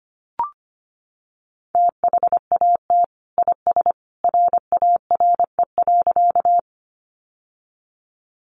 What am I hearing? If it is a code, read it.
Morse code, heard as a keyed tone, about 25 wpm, 700 Hz. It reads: THAT IS RARE.
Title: Sentences from Top 1100 Words 25wpm